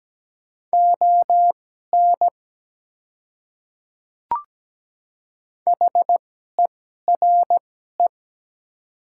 Morse code, heard as a keyed tone, about 17 wpm, spaced as for 12 wpm, 700 Hz. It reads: ON HERE
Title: 2x Character Spacing - Top 200 Words - 17wpm